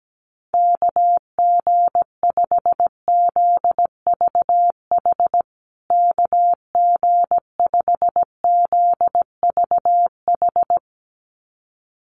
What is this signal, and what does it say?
Morse code, heard as a keyed tone, about 17 wpm, 700 Hz. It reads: KG5ZVH KG5ZVH
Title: USA Call Signs Repeated 2x 17wpm